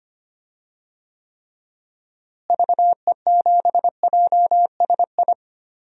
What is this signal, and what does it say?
Morse code, heard as a keyed tone, about 25 wpm, 700 Hz. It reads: VE7JSI
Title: DXCC-100 Call Signs - 6 Characters - Repeated 3x - 25wpm